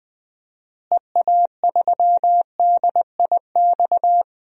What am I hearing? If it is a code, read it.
Morse code, heard as a keyed tone, about 20 wpm, 700 Hz. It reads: EA3DIX
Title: DXCC-100 Call Signs - 6 Characters - Repeated 3x - 20wpm